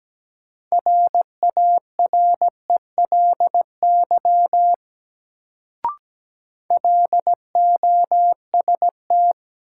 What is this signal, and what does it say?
Morse code, heard as a keyed tone, about 17 wpm, 700 Hz. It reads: RARELY LOST
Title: Sentences from Top 1400 Words - New Word Drill - 17wpm